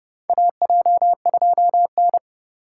Morse code, heard as a keyed tone, about 28 wpm, 700 Hz. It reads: AJ2N
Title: USA Call Signs - 4 Characters - 4x Character Spacing - 30wpm